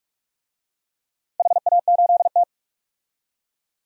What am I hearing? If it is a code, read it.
Morse code, heard as a keyed tone, about 44 wpm, 700 Hz. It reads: HA8T
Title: DXCC-100 Call Signs - 4 Characters - 45wpm